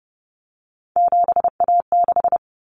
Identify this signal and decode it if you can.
Morse code, heard as a keyed tone, about 30 wpm, 700 Hz. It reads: 7A6
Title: Common Cut Numbers - 3-Digit 30wpm